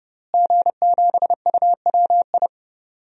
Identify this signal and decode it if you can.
Morse code, heard as a keyed tone, about 30 wpm, 700 Hz. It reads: G7UWI
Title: DXCC-100 Call Signs - 5 Characters - Repeated 3x - 30wpm